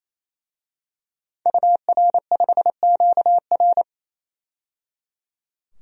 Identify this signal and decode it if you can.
Morse code, heard as a keyed tone, about 28 wpm, 700 Hz. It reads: UR5QR
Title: DXCC-100 Call Signs - 5 Characters - 28wpm